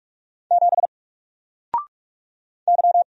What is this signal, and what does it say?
Morse code, heard as a keyed tone, about 46 wpm, 700 Hz. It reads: 7 Y
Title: Single Letter-Number 45wpm